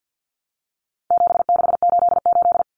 Morse code, heard as a keyed tone, about 50 wpm, 700 Hz. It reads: 7688
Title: Common Cut Numbers - 4-Digit 50wpm